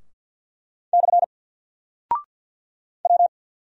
Morse code, heard as a keyed tone, about 48 wpm, 700 Hz. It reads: / W
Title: Single Character (Letter:Number:Punctuation:Prosign) Alt 50wpm